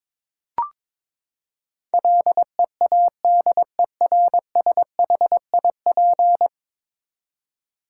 Morse code, heard as a keyed tone, about 21 wpm, 700 Hz. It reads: LEADERSHIP
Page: morse-code-ninja-practice-22wpm